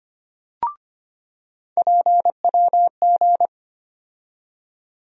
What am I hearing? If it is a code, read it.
Morse code, heard as a keyed tone, about 25 wpm, 700 Hz. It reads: PWG
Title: USA Call Sign Postfixes - 3 Letters 25wpm